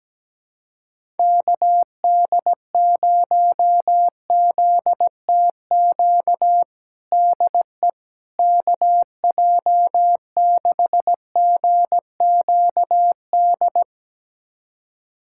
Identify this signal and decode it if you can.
Morse code, heard as a keyed tone, about 17 wpm, 700 Hz. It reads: KD0ZTQ DE KJ6GQD
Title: USA Callsign DE Callsign 17wpm